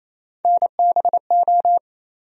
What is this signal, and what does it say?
Morse code, heard as a keyed tone, about 28 wpm, 700 Hz. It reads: NBO
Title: USA Call Sign Postfixes - 3 Letters 28wpm